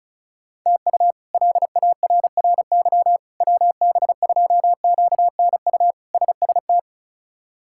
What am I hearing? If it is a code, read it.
Morse code, heard as a keyed tone, about 36 wpm, 700 Hz. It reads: TU LARRY WB2QNU SST
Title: K1USN SST QSO Practice - 35wpm